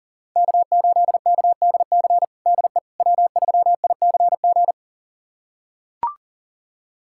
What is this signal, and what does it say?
Morse code, heard as a keyed tone, about 40 wpm, 700 Hz. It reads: K8KDC DE W3ICG
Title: USA Callsign DE Callsign - 5 Characters - 40wpm